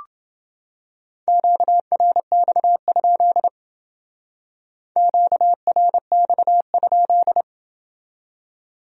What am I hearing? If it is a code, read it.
Morse code, heard as a keyed tone, about 29 wpm, 700 Hz. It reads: QRX? QRX?